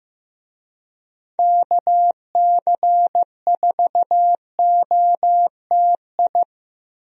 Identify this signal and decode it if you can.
Morse code, heard as a keyed tone, about 15 wpm, 700 Hz. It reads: KC4OTI
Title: USA Call Signs - 6 Characters - Repeated 2x - 15wpm